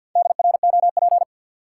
Morse code, heard as a keyed tone, about 52 wpm, 700 Hz. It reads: DROP
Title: Sentences from Top 1400 Words - New Word Drill - 50wpm